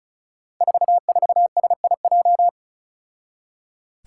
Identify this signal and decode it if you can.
Morse code, heard as a keyed tone, about 36 wpm, 700 Hz. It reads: 44SIJ
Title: Five Challenging Letter-Number Combinations 35wpm